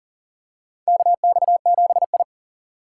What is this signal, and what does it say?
Morse code, heard as a keyed tone, about 39 wpm, 700 Hz. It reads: KX7I
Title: USA Call Signs - 4 Characters - 6x Character Spacing - 40wpm